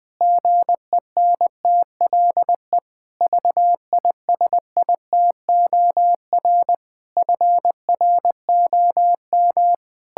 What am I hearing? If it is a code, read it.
Morse code, heard as a keyed tone, about 19 wpm, 700 Hz. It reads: GENTLE VISITOR FROM
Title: Pangram Challenge v2 - 20wpm